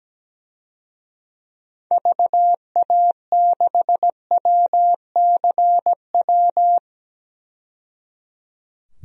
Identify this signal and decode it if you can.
Morse code, heard as a keyed tone, about 17 wpm, 700 Hz. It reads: VA6WCW
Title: DXCC-100 Call Signs - 6 Characters - 17wpm